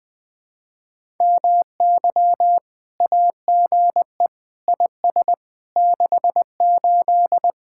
Morse code, heard as a keyed tone, about 20 wpm, 700 Hz. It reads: MY AGE IS 68
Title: Speed-Racing Common QSO Phrases - 17wpm